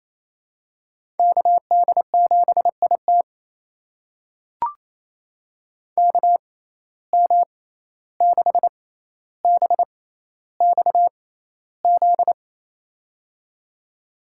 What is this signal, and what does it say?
Morse code, heard as a keyed tone, about 28 wpm, spaced as for 14 wpm, 700 Hz. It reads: KD7IT KM6BXZ
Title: USA Call Signs - 6x Character Spacing - 28wpm